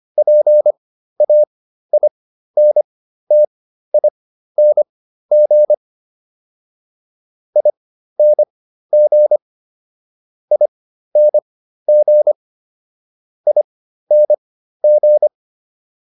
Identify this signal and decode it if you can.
Morse code, heard as a keyed tone, about 25 wpm, spaced as for 13 wpm, 600 Hz. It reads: PAINTING ING ING ING